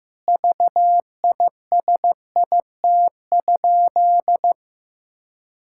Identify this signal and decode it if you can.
Morse code, heard as a keyed tone, about 15 wpm, 700 Hz. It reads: VISIT?
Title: Sentences from Top 1000 Words - Review Part 6 - 15wpm